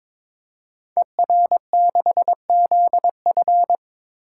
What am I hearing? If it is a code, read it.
Morse code, heard as a keyed tone, about 22 wpm, 700 Hz. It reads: ER6ZF
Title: Code Groups - 5 Letter:Numbers - 22wpm